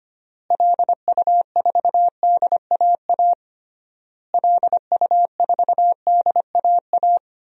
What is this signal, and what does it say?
Morse code, heard as a keyed tone, about 25 wpm, 700 Hz. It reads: LU4DAA LU4DAA
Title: DXCC-100 Call Signs - 6 Characters - Repeated 3x - 25wpm